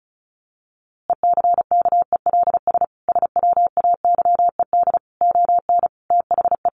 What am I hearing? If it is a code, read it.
Morse code, heard as a keyed tone, about 34 wpm, 700 Hz. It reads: ECKELS SWAYED ON THE
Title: A Sound Of Thunder 35wpm